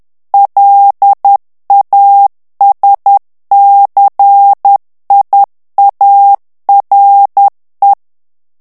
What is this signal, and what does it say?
Morse code, heard as a keyed tone, about 10 wpm, 800 Hz. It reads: LASCIARE
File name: morse mp3 / ITALIEN texte clair 1 vitesse 10 mots minute